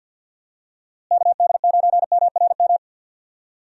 Morse code, heard as a keyed tone, about 49 wpm, 700 Hz. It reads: KD9MRM
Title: USA Call Signs - 6 Characters - Repeated 2x - 50wpm